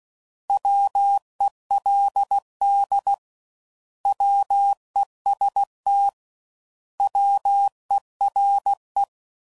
Morse code, heard as a keyed tone, about 16 wpm, 800 Hz. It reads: WELD WEST WERE
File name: Prefixes / By First Letter / Ws